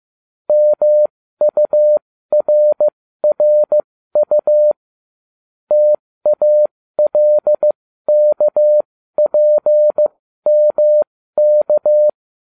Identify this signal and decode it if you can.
Morse code, heard as a keyed tone, about 15 wpm, spaced as for 13 wpm, 600 Hz. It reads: MURRU TALKPMK